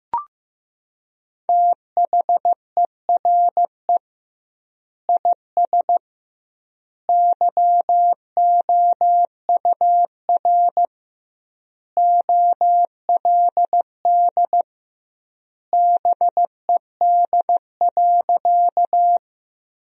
Morse code, heard as a keyed tone, about 15 wpm, 700 Hz. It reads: THERE IS YOUR OLD BED.
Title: 2x Word-Spacing - Sentences from Top 500 Words - 15wpm